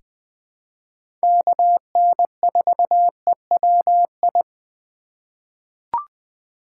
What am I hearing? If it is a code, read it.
Morse code, heard as a keyed tone, about 21 wpm, 700 Hz. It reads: KN4EWI
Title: K1USN SST QSO Practice - 20wpm